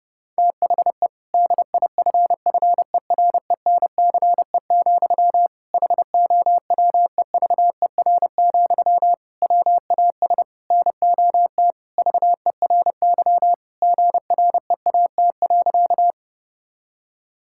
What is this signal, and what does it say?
Morse code, heard as a keyed tone, about 30 wpm, 700 Hz. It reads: THE DIFFERENCE, HOWEVER, WAS NOT VERY GREAT.
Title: Sentences from Top 1000 Words - Review Part 5 - 30wpm